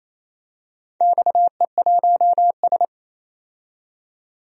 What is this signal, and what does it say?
Morse code, heard as a keyed tone, about 28 wpm, 700 Hz. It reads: XE1S
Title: DXCC-100 Call Signs - 4 Characters - 28wpm